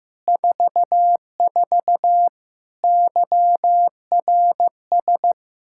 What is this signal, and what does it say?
Morse code, heard as a keyed tone, about 15 wpm, 700 Hz. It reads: 44 YRS
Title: Practice QSO - Exchange 3 - More Info - 15wpm